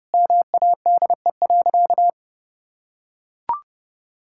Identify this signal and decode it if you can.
Morse code, heard as a keyed tone, about 30 wpm, 700 Hz. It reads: MADE.
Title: Sentences from Top 1000 Words - Review Part 5 - 30wpm